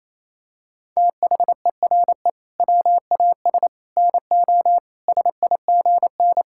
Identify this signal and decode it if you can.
Morse code, heard as a keyed tone, about 28 wpm, 700 Hz. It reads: THERE WAS NO SIGN
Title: Sentences from Top 1000 Words - Review Part 6 - 28wpm